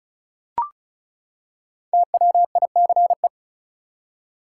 Morse code, heard as a keyed tone, about 36 wpm, 700 Hz. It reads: TWICE
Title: Five Letter Words - 35wpm